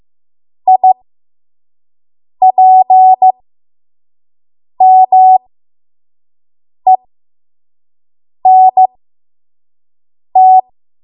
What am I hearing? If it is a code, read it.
Morse code, heard as a keyed tone, about 15 wpm, spaced as for 5 wpm, 750 Hz. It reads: IPMENT
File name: Morse / 05WPM